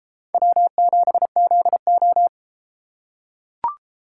Morse code, heard as a keyed tone, about 33 wpm, 700 Hz. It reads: W7ZO